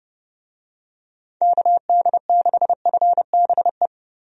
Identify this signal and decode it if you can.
Morse code, heard as a keyed tone, about 30 wpm, 700 Hz. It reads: KD6FBE
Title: USA Call Signs - 6 Characters - Repeated 2x - 30wpm